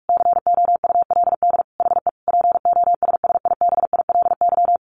Morse code, heard as a keyed tone, about 44 wpm, 700 Hz. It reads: COULD HE POSSIBILY
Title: Sentences from Top 1300 Words 45wpm